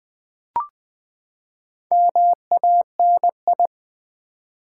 Morse code, heard as a keyed tone, about 20 wpm, 700 Hz. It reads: MANI
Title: Build A Word - 4 Letters At A Time - 12-Letter Words - 20wpm